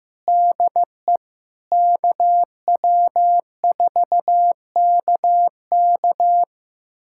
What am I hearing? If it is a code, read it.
Morse code, heard as a keyed tone, about 15 wpm, 700 Hz. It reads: DE KW4KK
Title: USA Callsign DE Callsign 15wpm